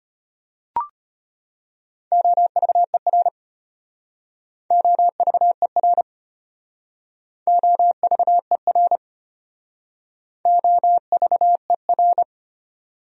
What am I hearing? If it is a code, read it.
Morse code, heard as a keyed tone, about 33 wpm, 700 Hz. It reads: OVER OVER OVER OVER